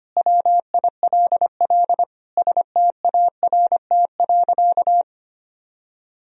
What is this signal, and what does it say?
Morse code, heard as a keyed tone, about 25 wpm, 700 Hz. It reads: WILL START.
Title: Sentences from Top 300 Words and Abbreviations 25wpm